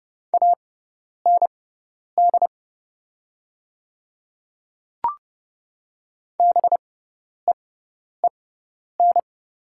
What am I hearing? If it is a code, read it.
Morse code, heard as a keyed tone, about 30 wpm, spaced as for 10 wpm, 700 Hz. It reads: AND BEEN
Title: 6x Character Spacing - Top 100 Words - 30wpm